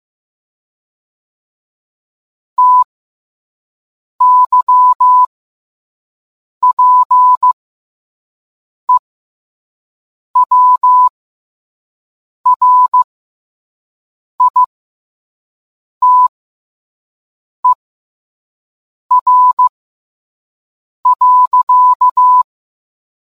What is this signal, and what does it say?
Morse code, heard as a keyed tone, about 15 wpm, spaced as for 5 wpm, 1000 Hz. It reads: TYPEWRITER.